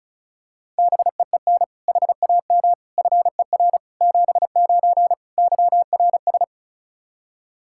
Morse code, heard as a keyed tone, about 36 wpm, 700 Hz. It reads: BEEN HAM FER 79 YRS
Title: Practice QSO - Exchange 3 - More Info - 35wpm